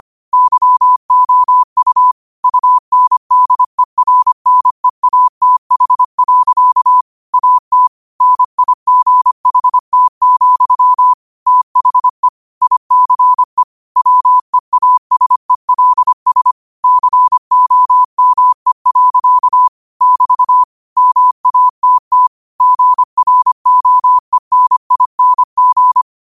Morse code, heard as a keyed tone, about 25 wpm, 1000 Hz. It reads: YOU UNDERNEATH. AT NIGHT, THE ICE WEASELS COME. <BT> MATT GROENING
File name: 25wpm